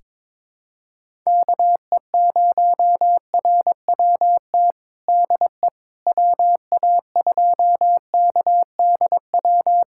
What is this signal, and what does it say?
Morse code, heard as a keyed tone, about 22 wpm, 700 Hz. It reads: KE0RWT DE WA2KDW
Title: USA Callsign DE Callsign - 6 Characters - 22wpm